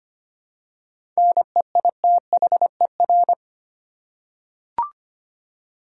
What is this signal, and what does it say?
Morse code, heard as a keyed tone, about 25 wpm, 700 Hz. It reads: NEITHER
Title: Seven Letter Words - 25wpm